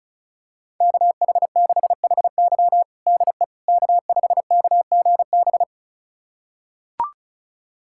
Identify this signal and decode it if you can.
Morse code, heard as a keyed tone, about 36 wpm, 700 Hz. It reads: KH6HY DE K5KGB
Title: USA Callsign DE Callsign - 5 Characters - 35wpm